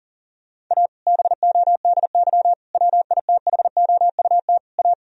Morse code, heard as a keyed tone, about 40 wpm, 700 Hz. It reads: A BODY WITHOUT A
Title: Sentences from Top 1000 Words - Review Part 3 - 40wpm